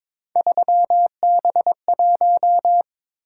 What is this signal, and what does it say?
Morse code, heard as a keyed tone, about 22 wpm, 700 Hz. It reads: 3B1
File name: practice / 22wpm / Three Letter-Number Combinations - 22wpm